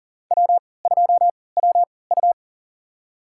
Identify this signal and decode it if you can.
Morse code, heard as a keyed tone, about 38 wpm, spaced as for 22 wpm, 700 Hz. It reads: W2WU
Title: USA Call Signs - 4 Characters - 3x Character Spacing - 40wpm